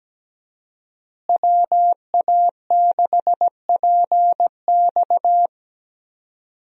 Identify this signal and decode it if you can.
Morse code, heard as a keyed tone, about 17 wpm, 700 Hz. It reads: WA6PX
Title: USA Call Signs - 5 Characters - Repeated 3x - 17wpm